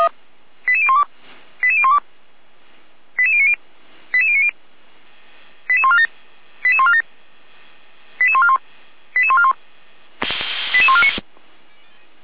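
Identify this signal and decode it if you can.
Five-tone selective call (ZVEI), heard as quick runs of five tones, tones of about 70 ms. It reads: 88012 88012 88090 88090 88147 88147 88141 88141 88149
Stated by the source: ZVEI five-tone call 88397 received on an SDR.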